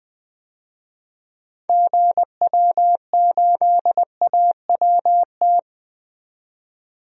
Morse code, heard as a keyed tone, about 20 wpm, 700 Hz. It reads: GW8AWT
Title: DXCC-100 Call Signs - 6 Characters - 20wpm